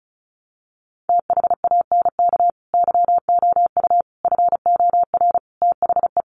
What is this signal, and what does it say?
Morse code, heard as a keyed tone, about 34 wpm, 700 Hz. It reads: THANK YOU FOR THE
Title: Sentences from Top 600 Words 35wpm